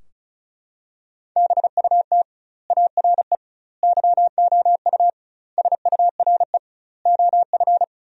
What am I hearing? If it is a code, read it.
Morse code, heard as a keyed tone, about 36 wpm, 700 Hz. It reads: BUT ARE YOU SURE OF
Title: 2x Word-Spacing - Sentences from Top 300 Words - 35wpm